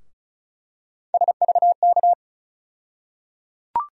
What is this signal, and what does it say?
Morse code, heard as a keyed tone, about 36 wpm, 700 Hz. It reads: SVK
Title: USA Call Sign Postfixes - 3 Letters 35wpm